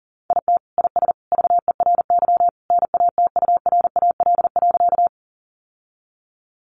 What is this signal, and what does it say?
Morse code, heard as a keyed tone, about 40 wpm, 700 Hz. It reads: IT IS VERY NATURAL.
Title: Sentences from Top 700 Words 40wpm